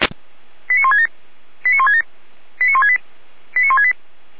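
Five-tone selective call (ZVEI), heard as quick runs of five tones, tones of about 70 ms. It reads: 89167 89167 89168 89168
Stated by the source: ZVEI five-tone selective call received with an SDR.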